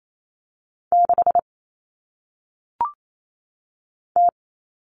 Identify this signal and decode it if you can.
Morse code, heard as a keyed tone, about 28 wpm, 700 Hz. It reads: 6 T
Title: All Cut Numbers - 1-Digit 28wpm